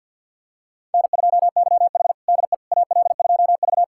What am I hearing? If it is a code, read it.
Morse code, heard as a keyed tone, about 50 wpm, 700 Hz. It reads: N2YH DE AL2V
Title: USA Callsign DE Callsign - 4 Characters - 50wpm